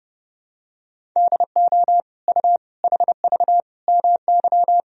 Morse code, heard as a keyed tone, about 31 wpm, 700 Hz. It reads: DO U HV MY
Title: Sentences from Top 500 Words and Abbreviations 30wpm